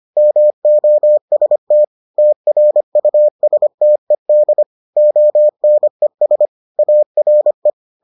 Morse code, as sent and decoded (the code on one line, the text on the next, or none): -- --- ... - / - .-. ..- ... - . -.. / --- -. ... / .- .-. .
MOST TRUSTED ONES ARE